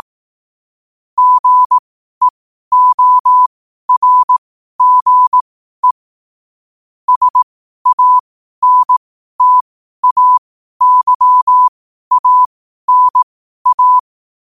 Morse code, heard as a keyed tone, about 18 wpm, spaced as for 12 wpm, 1000 Hz. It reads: GEORGE SANTAYANA